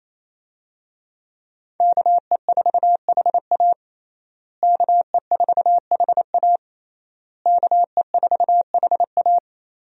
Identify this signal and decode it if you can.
Morse code, heard as a keyed tone, about 28 wpm, 700 Hz. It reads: KE4HA KE4HA KE4HA
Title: USA Call Signs - 5 Characters - Repeated 3x - 28wpm